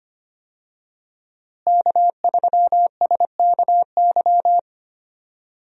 Morse code, heard as a keyed tone, about 25 wpm, 700 Hz. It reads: K3SKY
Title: USA Call Signs - 5 Characters - Repeated 2x - 25wpm